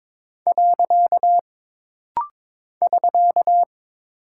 Morse code, heard as a keyed tone, about 22 wpm, 700 Hz. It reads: . <SK>
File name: practice / 22wpm / Single Punctuation-Prosign - Rapid-Fire - 22wpm